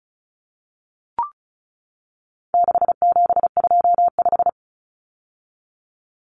Morse code, heard as a keyed tone, about 34 wpm, 700 Hz. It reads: 6725
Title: Numbers - 4-Digit 35wpm